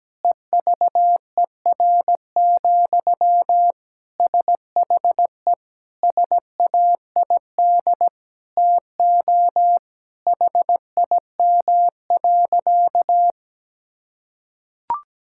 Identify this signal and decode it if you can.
Morse code, heard as a keyed tone, about 17 wpm, 700 Hz. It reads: EVER, SHE SAID TO HIM.
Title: Sentences from Top 1600 Words 17wpm